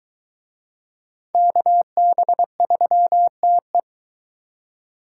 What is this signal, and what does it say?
Morse code, heard as a keyed tone, about 22 wpm, 700 Hz. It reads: KB3TE